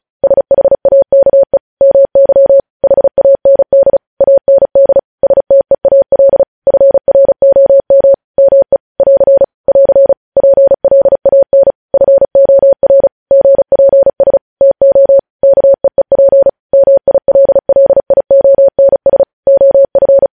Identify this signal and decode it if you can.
Morse code, heard as a keyed tone, about 36 wpm, 550 Hz. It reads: SHAKE MY HAND AND STEAL FROM ME <AR> <AR> PLAN FOR GPS TO KEEP MILLIONS OF
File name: podcast / News headlines-1x-35x35